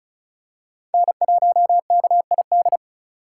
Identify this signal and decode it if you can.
Morse code, heard as a keyed tone, about 34 wpm, 700 Hz. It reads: N1KID
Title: USA Call Signs - 2x Character Spacing - 35wpm